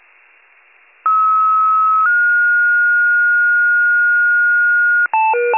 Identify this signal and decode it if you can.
Two-tone page, heard as a one-second tone, then a three-second tone, about 1290 then 1430 Hz. It is heard from 1.1 to 5.1 s.
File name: wcalert